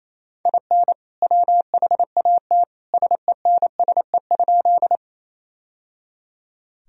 Morse code, heard as a keyed tone, about 28 wpm, 700 Hz. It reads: IN WHAT SENSE?
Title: Sentences from Top 1000 Words - Review Part 4 - 28wpm